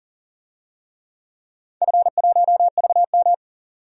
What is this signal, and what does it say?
Morse code, heard as a keyed tone, about 40 wpm, 700 Hz. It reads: F1VM
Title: DXCC-100 Call Signs - 4 Characters - 40wpm